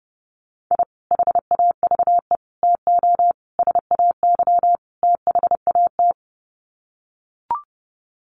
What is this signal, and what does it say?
Morse code, heard as a keyed tone, about 30 wpm, 700 Hz. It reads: I HAVE TO SAY THAT
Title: Sets of 5 Words Encore 30wpm